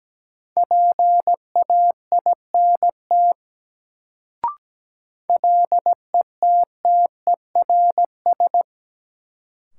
Morse code, heard as a keyed tone, about 17 wpm, 700 Hz. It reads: PAINT LETTERS
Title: Sentences from Top 1000 Words - Review Part 6 - New Word Drill - 17wpm